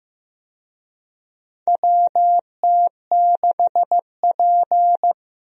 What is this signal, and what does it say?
Morse code, heard as a keyed tone, about 15 wpm, 700 Hz. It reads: WT6P
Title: USA Call Signs - 4 Characters - Repeated 2x - 15wpm